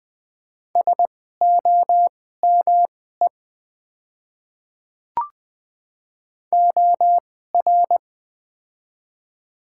Morse code, heard as a keyed tone, about 20 wpm, spaced as for 14 wpm, 700 Hz. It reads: SOME OR
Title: 2x Character Spacing - Top 100 Words - 20wpm